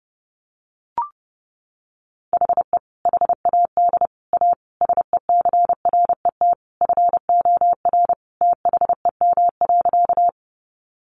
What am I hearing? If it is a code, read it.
Morse code, heard as a keyed tone, about 30 wpm, 700 Hz. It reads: HE HAD A SECRET FOR THEM.
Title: Sentences from Top 1200 Words 30wpm